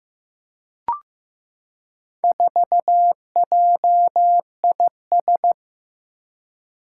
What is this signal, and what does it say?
Morse code, heard as a keyed tone, about 15 wpm, 700 Hz. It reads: 4JIS
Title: Four Challenging Letter-Number Combinations 15wpm